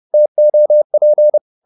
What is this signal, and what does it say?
Morse code, heard as a keyed tone, about 30 wpm, 600 Hz. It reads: TOP
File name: Korte ord DK 30wpm